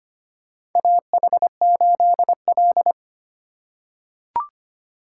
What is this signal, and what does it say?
Morse code, heard as a keyed tone, about 25 wpm, 700 Hz. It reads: AH8L
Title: USA Call Signs - 4 Characters - 2x Character Spacing - 25wpm